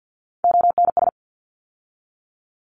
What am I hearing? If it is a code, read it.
Morse code, heard as a keyed tone, about 50 wpm, 700 Hz. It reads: ONS
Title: ARRL Sections 50wpm